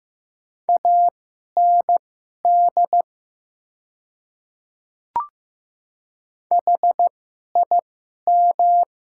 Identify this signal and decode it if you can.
Morse code, heard as a keyed tone, about 15 wpm, spaced as for 11 wpm, 700 Hz. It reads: AND HIM
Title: 2x Character Spacing - Top 200 Words - 15wpm